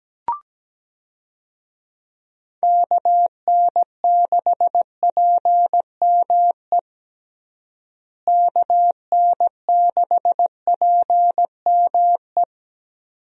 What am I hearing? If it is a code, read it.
Morse code, heard as a keyed tone, about 17 wpm, 700 Hz. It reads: KN6PME KN6PME
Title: USA Call Signs - 6 Characters - Repeated 2x - 17wpm